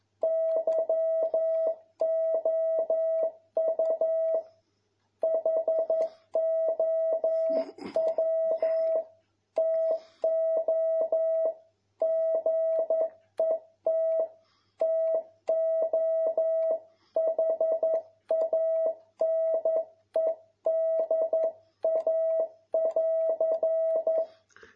Morse code, heard as a keyed tone, about 11 wpm, 650 Hz. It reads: YOU HOW TO GET TO HANEDA<AR>
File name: jh8cbh / 2021cw